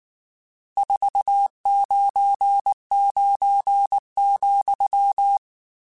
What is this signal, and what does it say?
Morse code, heard as a keyed tone, about 19 wpm, 800 Hz. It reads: 499,
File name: RSTs / RST-mp3s / Various RST @18